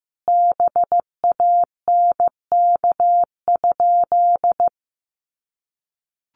Morse code, heard as a keyed tone, about 15 wpm, 700 Hz. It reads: BANK?
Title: Sentences from Top 1300 Words 15wpm